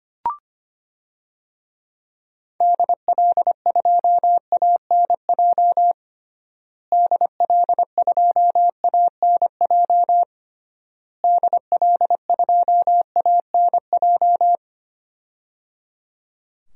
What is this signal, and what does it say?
Morse code, heard as a keyed tone, about 25 wpm, 700 Hz. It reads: DL2ANJ DL2ANJ DL2ANJ
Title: DXCC-100 Call Signs - 6 Characters - Repeated 3x - 25wpm